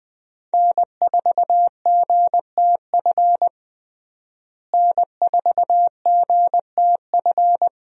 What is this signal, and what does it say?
Morse code, heard as a keyed tone, about 20 wpm, 700 Hz. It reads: N4GTF N4GTF
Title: USA Call Signs - 5 Characters - Repeated 2x - 20wpm